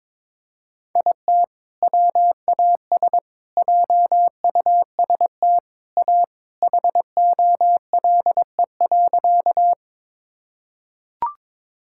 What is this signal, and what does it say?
Morse code, heard as a keyed tone, about 22 wpm, 700 Hz. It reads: IT WAS JUST A HOLE.
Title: Sentences from Top 1600 Words 22wpm